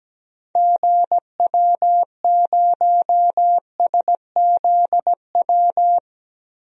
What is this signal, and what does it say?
Morse code, heard as a keyed tone, about 17 wpm, 700 Hz. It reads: GW0SZW
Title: DXCC-100 Call Signs - 6 Characters - 17wpm